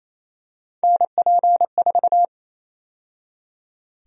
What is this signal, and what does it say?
Morse code, heard as a keyed tone, about 28 wpm, 700 Hz. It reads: NP4
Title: USA Call Sign Prefixes - 28wpm